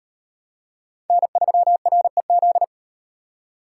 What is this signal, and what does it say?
Morse code, heard as a keyed tone, about 34 wpm, 700 Hz. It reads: N3REZ